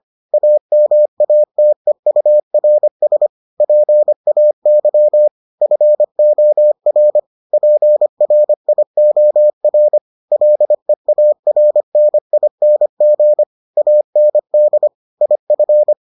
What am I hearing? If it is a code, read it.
Morse code, heard as a keyed tone, about 25 wpm, 600 Hz. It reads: AMATEURS PAY FOR PRIOR LEARNING AND IF